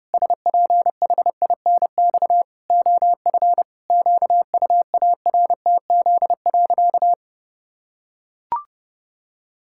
Morse code, heard as a keyed tone, about 30 wpm, 700 Hz. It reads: SPHINX OF QUARTZ.
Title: Pangram Challenge v2 - 30wpm